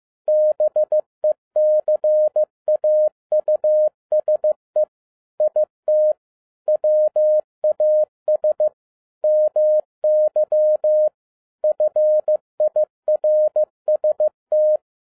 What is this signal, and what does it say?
Morse code, heard as a keyed tone, about 15 wpm, 610 Hz. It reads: BECAUSE IT WAS MY FIRST
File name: 2011-10-mp3 / n07-A DOT 15wpm.mp3